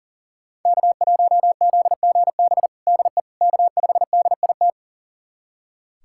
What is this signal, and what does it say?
Morse code, heard as a keyed tone, about 40 wpm, 700 Hz. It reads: K1ZGB DE K5DIT
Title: USA Callsign DE Callsign - 5 Characters - 40wpm